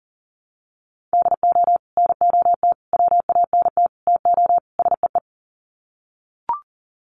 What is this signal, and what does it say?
Morse code, heard as a keyed tone, about 40 wpm, 700 Hz. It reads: DO NOT WANT TO SEE
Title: Sets of 5 Words 40wpm